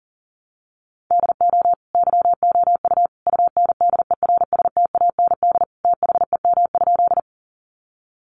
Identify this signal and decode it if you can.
Morse code, heard as a keyed tone, about 40 wpm, 700 Hz. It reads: DO YOU UNDERSTAND THEM?
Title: Sentences from Top 600 Words 40wpm